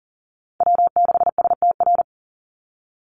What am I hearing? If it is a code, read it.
Morse code, heard as a keyed tone, about 40 wpm, 700 Hz. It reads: W6STR
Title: Call Signs Repeated 3x - 40wpm